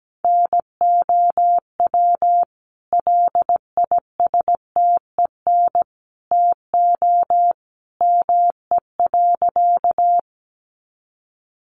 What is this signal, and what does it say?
Morse code, heard as a keyed tone, about 17 wpm, 700 Hz. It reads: NOW LISTEN TO ME.
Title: Sentences from Top 400 Words 17wpm